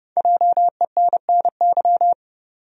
Morse code, heard as a keyed tone, about 30 wpm, 700 Hz. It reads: JENNY
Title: USA Common Names - 30wpm